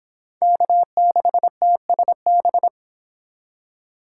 Morse code, heard as a keyed tone, about 26 wpm, 700 Hz. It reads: K6TSB